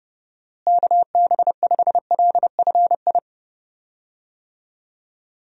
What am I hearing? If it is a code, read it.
Morse code, heard as a keyed tone, about 32 wpm, 700 Hz. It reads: KB5LFI
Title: USA Call Signs - 6 Characters - Repeated 3x - 30wpm